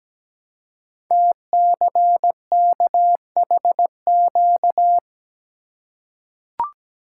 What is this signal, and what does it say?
Morse code, heard as a keyed tone, about 17 wpm, 700 Hz. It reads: TCKHQ
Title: Code Groups - 5 Letters - 17wpm